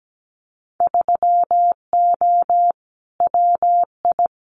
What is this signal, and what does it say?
Morse code, heard as a keyed tone, about 17 wpm, 700 Hz. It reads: 3O WI
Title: Winter Field Day Exchanges 17wpm